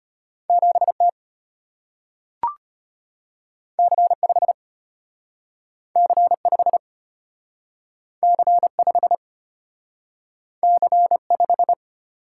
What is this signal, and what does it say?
Morse code, heard as a keyed tone, about 34 wpm, 700 Hz. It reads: 7T C5 C5 C5 C5